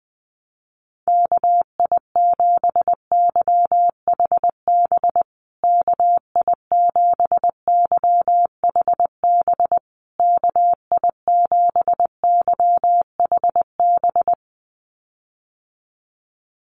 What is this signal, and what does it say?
Morse code, heard as a keyed tone, about 20 wpm, 700 Hz. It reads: KI7YHB KI7YHB KI7YHB
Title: USA Call Signs Repeated 3x 20wpm